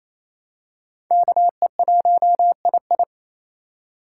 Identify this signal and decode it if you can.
Morse code, heard as a keyed tone, about 28 wpm, 700 Hz. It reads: KE1II
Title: USA Call Signs - 5 Characters - 4x Character Spacing - 28wpm